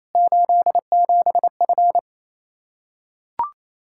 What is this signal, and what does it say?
Morse code, heard as a keyed tone, about 28 wpm, 700 Hz. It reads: 87F
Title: Practice QSO - Exchange 2 - Info - 28wpm